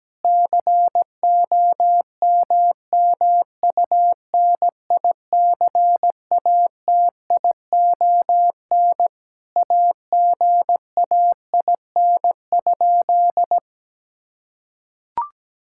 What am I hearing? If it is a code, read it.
Morse code, heard as a keyed tone, about 17 wpm, 700 Hz. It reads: COMMUNICATION AGAIN?
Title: Sentences from Top 1000 Words - Review Part 6 - 17wpm